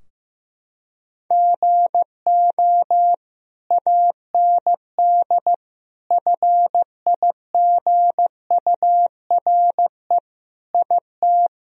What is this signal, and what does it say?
Morse code, heard as a keyed tone, about 15 wpm, 700 Hz. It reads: GO AND FIGURE IT
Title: Sentences from Top 1000 Words - Review Part 2 - 15wpm